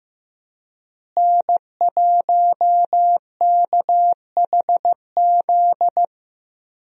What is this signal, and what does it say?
Morse code, heard as a keyed tone, about 15 wpm, 700 Hz. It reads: N1KHZ
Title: USA Call Signs - 5 Characters - Repeated 2x - 15wpm